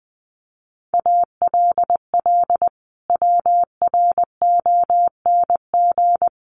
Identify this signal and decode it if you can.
Morse code, heard as a keyed tone, about 19 wpm, 700 Hz. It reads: ALL WRONG
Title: Sentences from Top 900 Words 20wpm